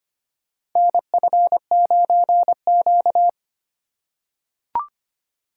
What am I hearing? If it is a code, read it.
Morse code, heard as a keyed tone, about 25 wpm, 700 Hz. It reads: NF9Q
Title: USA Call Signs - 4x Character Spacing - 25wpm